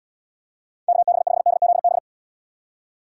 Morse code, heard as a keyed tone, about 75 wpm, 700 Hz. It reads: KB5DZB